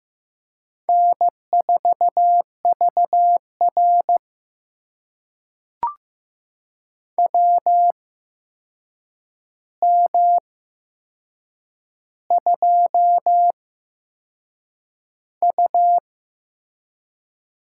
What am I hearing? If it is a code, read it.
Morse code, heard as a keyed tone, about 15 wpm, spaced as for 6 wpm, 700 Hz. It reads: N4VR WM2U
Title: USA Call Signs - 4 Characters - 8x Character Spacing - 15wpm